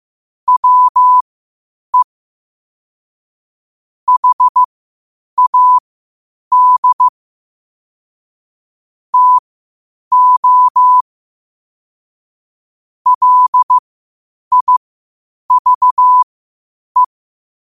Morse code, heard as a keyed tone, about 15 wpm, spaced as for 8 wpm, 1000 Hz. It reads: WE HAD TO LIVE
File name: qotdfiles / 8wpm